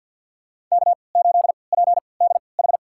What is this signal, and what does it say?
Morse code, heard as a keyed tone, about 51 wpm, spaced as for 28 wpm, 700 Hz. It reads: K8PDH
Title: USA Call Signs - 3x Character Spacing - 50wpm